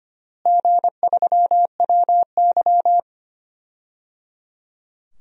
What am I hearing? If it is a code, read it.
Morse code, heard as a keyed tone, about 25 wpm, 700 Hz. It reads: G3WY
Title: DXCC-100 Call Signs - 4 Characters - Repeated 3x - 25wpm